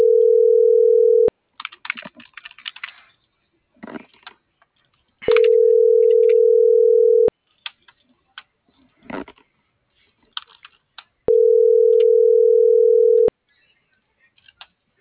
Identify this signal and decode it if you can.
Ringback tone (North America), 440 + 480 Hz.